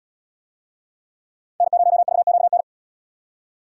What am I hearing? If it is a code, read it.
Morse code, heard as a keyed tone, about 67 wpm, 700 Hz. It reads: N0VQN